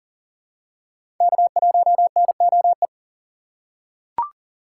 Morse code, heard as a keyed tone, about 40 wpm, 700 Hz. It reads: K1NOE
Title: USA Call Signs - 5 Characters - 4x Character Spacing - 40wpm